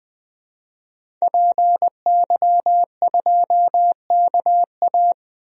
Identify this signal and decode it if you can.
Morse code, heard as a keyed tone, about 20 wpm, 700 Hz. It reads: PY2KA
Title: DXCC-100 Call Signs - 5 Characters - Repeated 2x - 20wpm